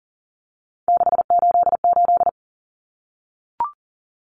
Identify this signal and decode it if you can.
Morse code, heard as a keyed tone, about 40 wpm, 700 Hz. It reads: 688
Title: Numbers - 3-Digit 40wpm